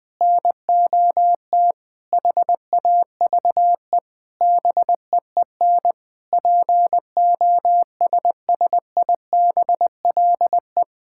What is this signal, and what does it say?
Morse code, heard as a keyed tone, about 20 wpm, 700 Hz. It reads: NOT HAVE BEEN POSSIBLE
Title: Sentences with 7-Word Phrases - 20wpm